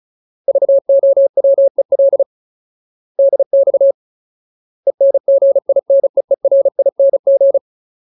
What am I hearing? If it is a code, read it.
Morse code, heard as a keyed tone, about 34 wpm, 550 Hz. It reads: VOWEL DX ENGINEERING
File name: Ultimate Word List / 35WPM Ultimate